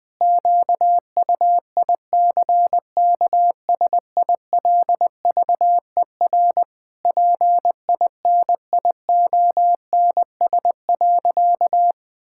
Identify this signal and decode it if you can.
Morse code, heard as a keyed tone, about 19 wpm, 700 Hz. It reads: QUICKSILVER PINIONS.
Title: Pangram Challenge v2 - 20wpm